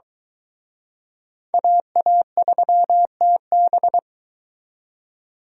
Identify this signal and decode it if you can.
Morse code, heard as a keyed tone, about 23 wpm, 700 Hz. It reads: AA3TB